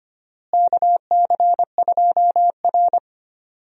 Morse code, heard as a keyed tone, about 25 wpm, 700 Hz. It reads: KC2R
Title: USA Call Signs - 4 Characters - Repeated 2x - 25wpm